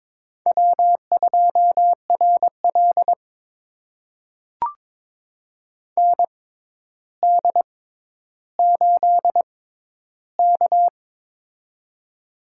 Morse code, heard as a keyed tone, about 22 wpm, spaced as for 11 wpm, 700 Hz. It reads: W2RL ND8K
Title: USA Call Signs - 4 Characters - 6x Character Spacing - 22wpm